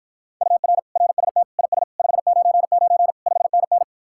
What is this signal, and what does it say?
Morse code, heard as a keyed tone, about 53 wpm, 700 Hz. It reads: UR RST IS 599 5NN